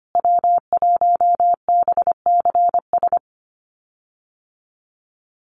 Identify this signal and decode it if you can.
Morse code, heard as a keyed tone, about 25 wpm, 700 Hz. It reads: W1BCS
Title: USA Call Signs Repeated 3x 25wpm